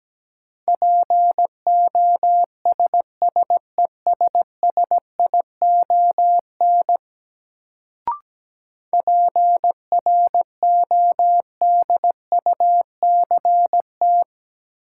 Morse code, heard as a keyed tone, about 17 wpm, 700 Hz. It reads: POSSESSION PRODUCT
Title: Sentences from Top 1400 Words - New Word Drill Alt - 17wpm